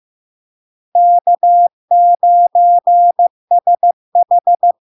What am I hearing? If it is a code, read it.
Morse code, heard as a keyed tone, about 15 wpm, 700 Hz. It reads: K9SH
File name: LARK-FD-2022-Log-in-CW-15wpm-700-Hz